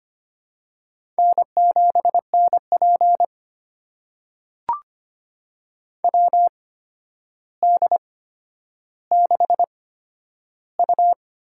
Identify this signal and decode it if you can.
Morse code, heard as a keyed tone, about 25 wpm, spaced as for 11 wpm, 700 Hz. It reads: N7NP WD6U
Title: USA Call Signs - 4 Characters - 8x Character Spacing - 25wpm